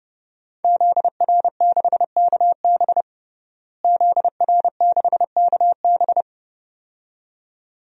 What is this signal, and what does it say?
Morse code, heard as a keyed tone, about 30 wpm, 700 Hz. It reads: ZR6KB ZR6KB
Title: DXCC-100 Call Signs - 5 Characters - Repeated 2x - 30wpm